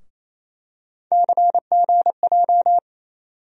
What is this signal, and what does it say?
Morse code, heard as a keyed tone, about 28 wpm, 700 Hz. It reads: CGJ
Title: USA Call Sign Postfixes - 3 Letters 28wpm